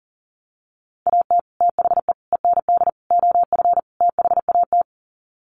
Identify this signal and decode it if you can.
Morse code, heard as a keyed tone, about 40 wpm, 700 Hz. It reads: AT THE END OF THAT
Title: Sets of 5 Words Encore2 40wpm